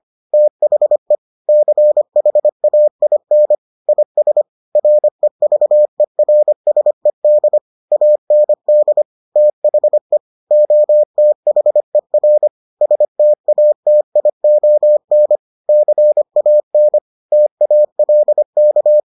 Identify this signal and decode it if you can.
Morse code, heard as a keyed tone, about 25 wpm, 600 Hz. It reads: THE CHAIN IS REVERSED AND THE OTHER STATION CAN TALK